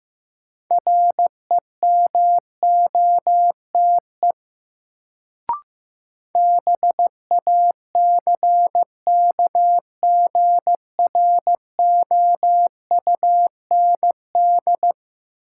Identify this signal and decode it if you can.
Morse code, heard as a keyed tone, about 15 wpm, 700 Hz. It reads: REMOTE BACKGROUND
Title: Sentences from Top 1300 Words - New Word Drill Alt - 15wpm